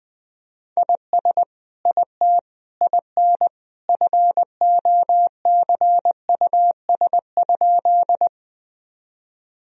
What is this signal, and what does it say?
Morse code, heard as a keyed tone, about 20 wpm, 700 Hz. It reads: IS IT IN FOCUS?
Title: Sentences from Top 1000 Words - Review Part 4 - 20wpm